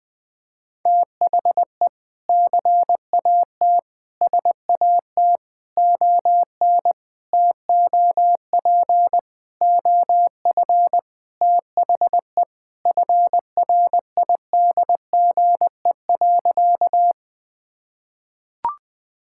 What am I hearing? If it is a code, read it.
Morse code, heard as a keyed tone, about 20 wpm, 700 Hz. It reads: THE CAT SAT ON TOP OF THE FRIDGE.
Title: Sentences with 4-Word Phrases - 20wpm